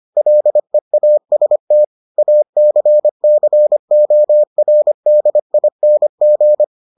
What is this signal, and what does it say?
Morse code, heard as a keyed tone, about 25 wpm, 600 Hz. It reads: LEAST ACCORDING